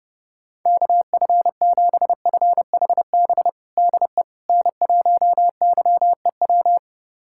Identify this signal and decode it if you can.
Morse code, heard as a keyed tone, about 32 wpm, 700 Hz. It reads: KF7FHB DE N1YEW
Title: USA Callsign DE Callsign 30wpm